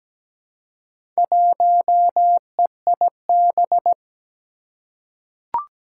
Morse code, heard as a keyed tone, about 17 wpm, 700 Hz. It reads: 1EIB
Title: Four Challenging Letter-Number Combinations 17wpm